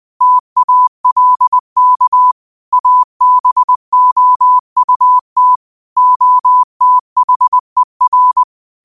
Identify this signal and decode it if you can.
Morse code, heard as a keyed tone, about 20 wpm, 1000 Hz. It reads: TALK ABOUT OTHER
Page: New quotes every day in morse code at 20 Words per minute.